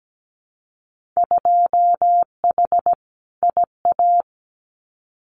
Morse code, heard as a keyed tone, about 17 wpm, 700 Hz. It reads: 2H IA
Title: Winter Field Day Exchanges Encore 17wpm